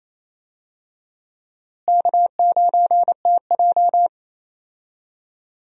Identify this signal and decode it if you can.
Morse code, heard as a keyed tone, about 28 wpm, 700 Hz. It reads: K9TJ
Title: USA Call Signs - 4 Characters - Repeated 2x - 28wpm